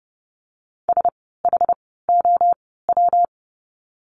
Morse code, heard as a keyed tone, about 30 wpm, spaced as for 17 wpm, 700 Hz. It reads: SHOW
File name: course / 30:10wpm / 115 Lesson 19f - Letter w - Words 30wpm